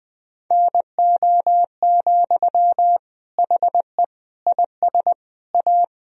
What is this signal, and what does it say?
Morse code, heard as a keyed tone, about 21 wpm, 700 Hz. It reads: NO, HE IS A
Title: Sentences from Top 1000 Words - Review Part 2 - 20wpm